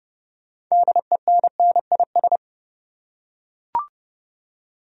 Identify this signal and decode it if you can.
Morse code, heard as a keyed tone, about 30 wpm, 700 Hz. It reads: DENNIS
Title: USA Common Names - 30wpm